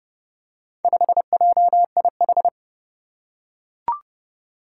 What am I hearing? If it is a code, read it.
Morse code, heard as a keyed tone, about 30 wpm, 700 Hz. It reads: 5JIH
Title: Four Challenging Letter-Number Combinations 30wpm